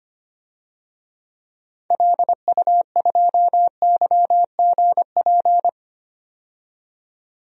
Morse code, heard as a keyed tone, about 25 wpm, 700 Hz. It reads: LU2YGP
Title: DXCC-100 Call Signs - 6 Characters - 25wpm